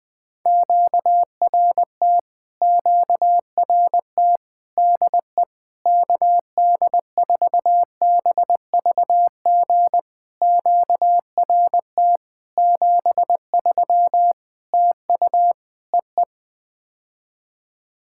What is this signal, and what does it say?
Morse code, heard as a keyed tone, about 20 wpm, 700 Hz. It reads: QRT QRT DE KD4BVG QRT 73 TU EE
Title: POTA QSO Practice - 20wpm